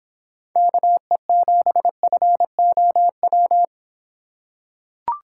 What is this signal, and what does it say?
Morse code, heard as a keyed tone, about 26 wpm, 700 Hz. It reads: KE7FOW